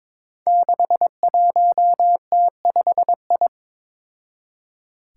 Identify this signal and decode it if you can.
Morse code, heard as a keyed tone, about 21 wpm, 700 Hz. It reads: 61T5I
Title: Five Challenging Letter-Number Combinations 22wpm